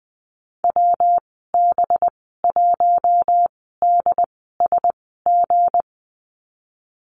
Morse code, heard as a keyed tone, about 20 wpm, spaced as for 15 wpm, 700 Hz. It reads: WB1DSG
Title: 213 Lesson 34g - Number 8 - Call Signs 20wpm